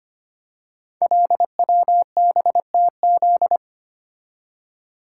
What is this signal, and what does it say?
Morse code, heard as a keyed tone, about 25 wpm, 700 Hz. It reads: LWBTZ
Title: Code Groups - 5 Letter:Numbers - 25wpm